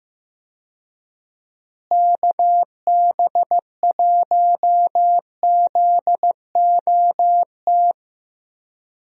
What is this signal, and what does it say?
Morse code, heard as a keyed tone, about 15 wpm, 700 Hz. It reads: KB1ZOT
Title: USA Call Signs - 6 Characters - Repeated 3x - 15wpm